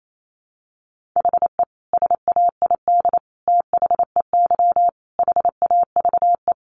Morse code, heard as a keyed tone, about 28 wpm, 700 Hz. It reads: HE SAID THEY HAVE
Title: Sentences from Top 800 Words 28wpm